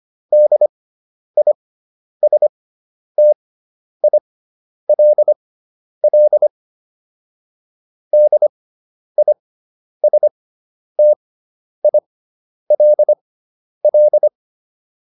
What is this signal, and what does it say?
Morse code, heard as a keyed tone, about 25 wpm, spaced as for 10 wpm, 600 Hz. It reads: DISTILL DISTILL